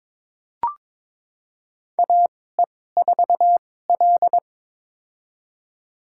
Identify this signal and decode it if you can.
Morse code, heard as a keyed tone, about 22 wpm, spaced as for 16 wpm, 700 Hz. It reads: AE4L
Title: USA Call Signs - 2x Character Spacing - 22wpm